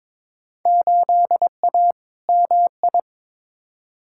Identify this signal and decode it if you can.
Morse code, heard as a keyed tone, about 22 wpm, 700 Hz. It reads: 8A MI
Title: Field Day Exchanges Encore - 22wpm